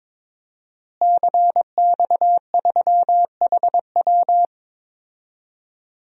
Morse code, heard as a keyed tone, about 21 wpm, 700 Hz. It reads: CX3HW
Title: DXCC-100 Call Signs - 5 Characters - Repeated 3x - 22wpm